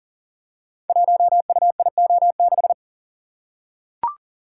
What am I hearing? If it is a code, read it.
Morse code, heard as a keyed tone, about 42 wpm, 700 Hz. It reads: 1UIO6
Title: Code Groups - 5 Letter:Numbers - 40wpm